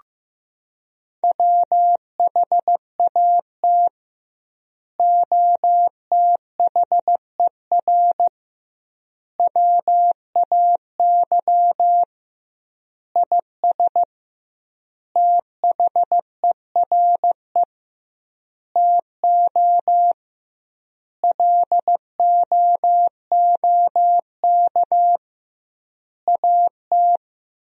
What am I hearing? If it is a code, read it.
Morse code, heard as a keyed tone, about 15 wpm, 700 Hz. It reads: WHAT OTHER WAY IS THERE TO LOOK AT
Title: 2x Word-Spacing - Sentences from Top 100 Words - 15wpm